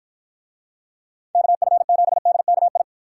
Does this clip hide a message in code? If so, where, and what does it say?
Yes: Morse code from 1.4 to 2.8 s, heard as a keyed tone, about 49 wpm, 700 Hz. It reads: KF7DKI